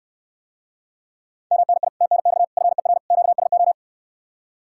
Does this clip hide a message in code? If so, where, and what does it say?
Yes: Morse code from 1.5 to 3.7 s, heard as a keyed tone, about 69 wpm, 700 Hz. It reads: MNI TNX FER QSO